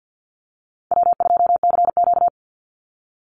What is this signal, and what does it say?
Morse code, heard as a keyed tone, about 50 wpm, 700 Hz. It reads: W2CQ
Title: Easy Call Signs Repeated 3x - 50wpm